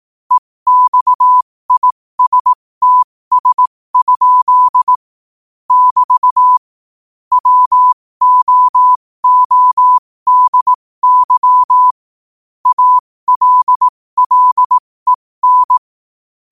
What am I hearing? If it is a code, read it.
Morse code, heard as a keyed tone, about 18 wpm, spaced as for 15 wpm, 1000 Hz. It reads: EXISTS? <BT> WOODY ALLEN